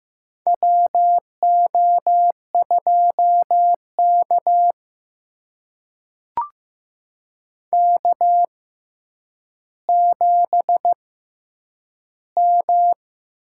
Morse code, heard as a keyed tone, about 15 wpm, spaced as for 9 wpm, 700 Hz. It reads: WO2K K7M
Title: USA Call Signs - 4 Characters - 6x Character Spacing - 15wpm